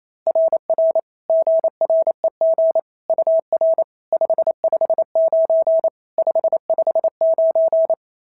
Morse code, heard as a keyed tone, about 25 wpm, 650 Hz. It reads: RR GREG UR 559 559
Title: SOTA QSO Practice 28wpm